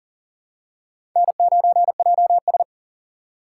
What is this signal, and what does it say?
Morse code, heard as a keyed tone, about 41 wpm, 700 Hz. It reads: N9JS
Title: USA Call Signs - 4 Characters - 4x Character Spacing - 40wpm